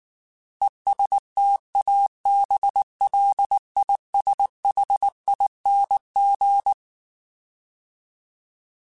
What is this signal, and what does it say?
Morse code, heard as a keyed tone, about 19 wpm, 800 Hz. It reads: ESTABLISHING
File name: Alphabetical / Alphabetical mp3s / Es @18